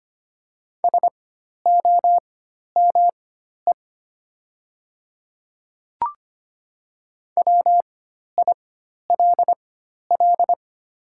Morse code, heard as a keyed tone, about 25 wpm, spaced as for 12 wpm, 700 Hz. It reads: SOME WILL
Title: 4x Character Spacing - Top 100 Words - 25wpm